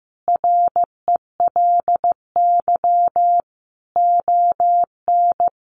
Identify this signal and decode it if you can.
Morse code, heard as a keyed tone, about 15 wpm, 700 Hz. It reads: RELY ON
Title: Sentences from Top 1200 Words 15wpm